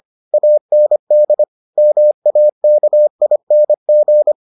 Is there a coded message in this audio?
Yes: Morse code from 0.3 to 4.3 s, heard as a keyed tone, about 25 wpm, 600 Hz. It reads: AND MAKING